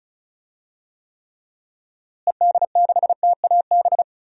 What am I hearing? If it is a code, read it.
Morse code, heard as a keyed tone, about 36 wpm, 700 Hz. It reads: ED6TAB
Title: DXCC-100 Call Signs - 6 Characters - Repeated 3x - 35wpm